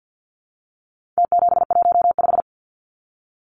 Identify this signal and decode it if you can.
Morse code, heard as a keyed tone, about 49 wpm, 700 Hz. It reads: T715
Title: Numbers - Contest Style - 4-Digit 50wpm